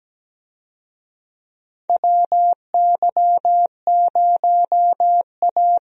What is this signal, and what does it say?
Morse code, heard as a keyed tone, about 17 wpm, 700 Hz. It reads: WY0A
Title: USA Call Signs - 4 Characters - Repeated 2x - 17wpm